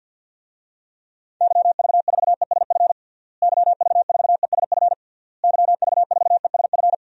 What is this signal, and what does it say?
Morse code, heard as a keyed tone, about 51 wpm, 700 Hz. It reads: YV4ESF YV4ESF YV4ESF
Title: DXCC-100 Call Signs - 6 Characters - Repeated 3x - 50wpm